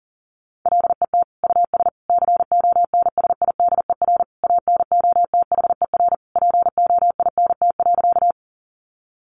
Morse code, heard as a keyed tone, about 37 wpm, 700 Hz. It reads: LET US CONSIDER ANOTHER POINT.
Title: Sentences from Top 600 Words 40wpm